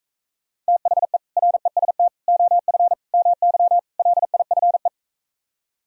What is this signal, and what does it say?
Morse code, heard as a keyed tone, about 45 wpm, 700 Hz. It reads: THE REST OF MY LIFE